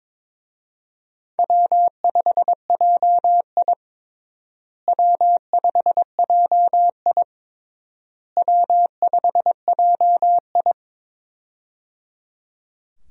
Morse code, heard as a keyed tone, about 22 wpm, 700 Hz. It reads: W5JI W5JI W5JI
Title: USA Call Signs - 4 Characters - Repeated 3x - 22wpm